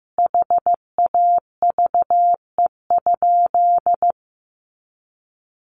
Morse code, heard as a keyed tone, about 15 wpm, 700 Hz. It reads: HAVE?
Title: Sentences from Top 1200 Words 15wpm